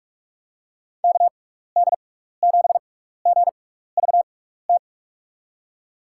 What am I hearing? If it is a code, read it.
Morse code, heard as a keyed tone, about 47 wpm, spaced as for 16 wpm, 700 Hz. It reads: KD7GVT
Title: USA Call Signs - 6x Character Spacing - 45wpm